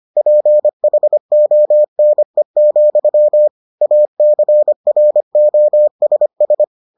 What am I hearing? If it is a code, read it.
Morse code, heard as a keyed tone, about 25 wpm, 600 Hz. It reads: PHONE, ACROSS